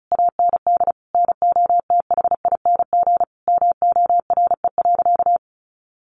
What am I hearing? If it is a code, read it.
Morse code, heard as a keyed tone, about 34 wpm, 700 Hz. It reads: AND NOTHING MORE.
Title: Sentences from Top 800 Words 35wpm